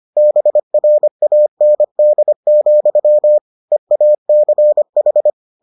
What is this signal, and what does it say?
Morse code, heard as a keyed tone, about 25 wpm, 600 Hz. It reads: BRAND, EACH